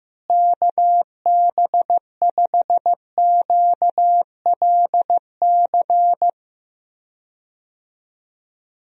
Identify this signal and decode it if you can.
Morse code, heard as a keyed tone, about 15 wpm, 700 Hz. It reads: KB5QLC
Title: USA Call Signs - 6 Characters - Repeated 2x - 15wpm